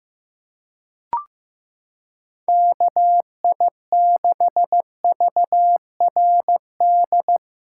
Morse code, heard as a keyed tone, about 15 wpm, 700 Hz. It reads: KI6VRD
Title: USA Callsign DE Callsign 15wpm